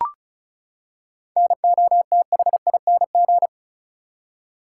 Morse code, heard as a keyed tone, about 34 wpm, 700 Hz. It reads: NOTHING
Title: Seven Letter Words - 35wpm